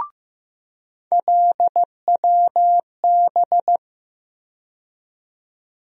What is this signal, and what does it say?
Morse code, heard as a keyed tone, about 15 wpm, 700 Hz. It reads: LWB
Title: USA Call Sign Postfixes - 3 Letters 15wpm